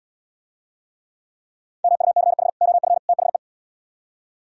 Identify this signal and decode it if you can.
Morse code, heard as a keyed tone, about 76 wpm, 700 Hz. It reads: MUCH OF THE